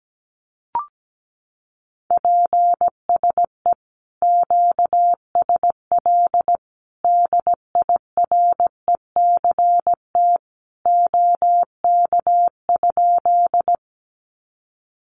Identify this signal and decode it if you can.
Morse code, heard as a keyed tone, about 17 wpm, 700 Hz. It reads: PSE QSL DIRECT OK?
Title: Common QSO Phrases 17wpm